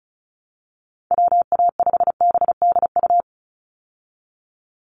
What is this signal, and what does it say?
Morse code, heard as a keyed tone, about 34 wpm, 700 Hz. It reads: WA5BDU
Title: Call Signs Repeated 2x - 35wpm